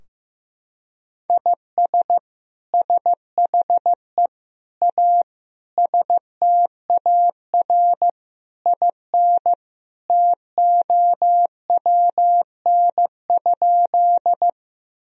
Morse code, heard as a keyed tone, about 15 wpm, 700 Hz. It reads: IS SHE A STAR IN TOWN?
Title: Sentences from Top 1000 Words - Review Part 2 - 15wpm